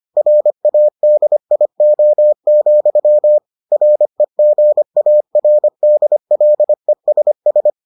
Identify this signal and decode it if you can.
Morse code, heard as a keyed tone, about 25 wpm, 600 Hz. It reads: RADIO, REGARDLESS